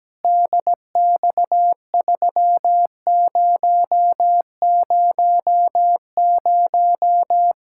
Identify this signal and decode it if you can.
Morse code, heard as a keyed tone, about 17 wpm, 700 Hz. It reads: DX3000
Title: K1USN SST QSO Practice 17wpm